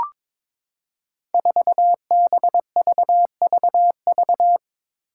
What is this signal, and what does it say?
Morse code, heard as a keyed tone, about 22 wpm, 700 Hz. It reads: 4BVVV
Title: Five Challenging Letter-Number Combinations 22wpm